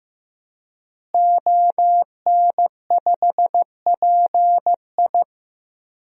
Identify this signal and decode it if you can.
Morse code, heard as a keyed tone, about 15 wpm, 700 Hz. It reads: ON5PI
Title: DXCC-100 Call Signs - 5 Characters - Repeated 3x - 15wpm